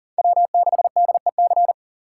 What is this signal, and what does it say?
Morse code, heard as a keyed tone, about 40 wpm, 700 Hz. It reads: W6DEC
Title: USA Call Signs - 5 Characters - 6x Character Spacing - 40wpm